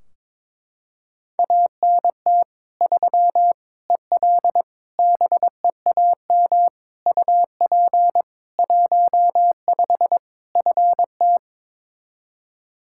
Morse code, heard as a keyed tone, about 22 wpm, 700 Hz. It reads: ANT 3 EL BEAM UP 15 FT
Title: Practice QSO - Exchange 2 - Info - 22wpm